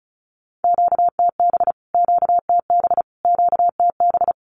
Morse code, heard as a keyed tone, about 34 wpm, 700 Hz. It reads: QTB QTB QTB
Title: Q-Codes Repeated 3x 35wpm